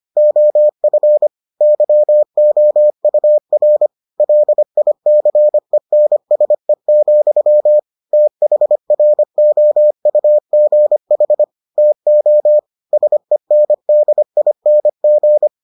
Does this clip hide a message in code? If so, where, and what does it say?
Yes: Morse code from 0.2 to 15.5 s, heard as a keyed tone, about 25 wpm, 600 Hz. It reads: OF YOUR LICENSE, THROUGH TO SENDING